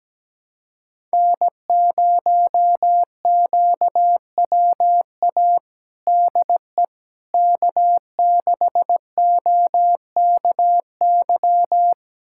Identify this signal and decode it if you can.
Morse code, heard as a keyed tone, about 17 wpm, 700 Hz. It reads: N0QWA DE K6OKY
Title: USA Callsign DE Callsign - 5 Characters - 17wpm